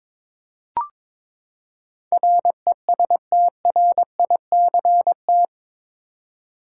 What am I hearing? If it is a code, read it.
Morse code, heard as a keyed tone, about 21 wpm, 700 Hz. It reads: RESTRICT
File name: practice / 22wpm / Eight Letter Words - 22wpm